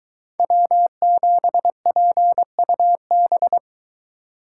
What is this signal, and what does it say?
Morse code, heard as a keyed tone, about 22 wpm, 700 Hz. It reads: W7PUB